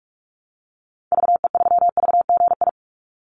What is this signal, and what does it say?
Morse code, heard as a keyed tone, about 44 wpm, 700 Hz. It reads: VE3VGI
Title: Call Signs Repeated 2x - 45wpm